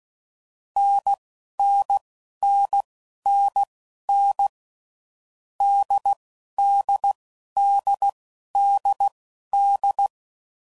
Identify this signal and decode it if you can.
Morse code, heard as a keyed tone, about 16 wpm, spaced as for 11 wpm, 800 Hz. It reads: NNNNN DDDDD
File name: Combinations / Difficult Combos / NDB NDB6